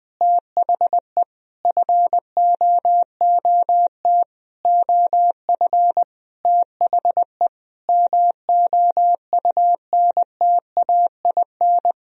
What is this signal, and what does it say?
Morse code, heard as a keyed tone, about 20 wpm, 700 Hz. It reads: THE FOOT OF THE MOUNTAIN
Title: Sentences from Top 1000 Words - Review Part 3 - 20wpm